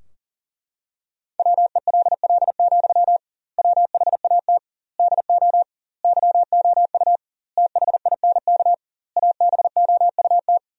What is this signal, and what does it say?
Morse code, heard as a keyed tone, about 40 wpm, 700 Hz. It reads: WELL, WHAT DO YOU THINK ABOUT
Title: 2x Word-Spacing - Sentences from Top 200 Words - 40wpm